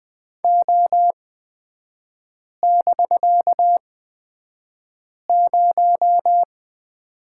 Morse code, heard as a keyed tone, about 20 wpm, spaced as for 5 wpm, 700 Hz. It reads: O<BK>0
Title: 250 Lesson 40e - BK - Three Characters 20wpm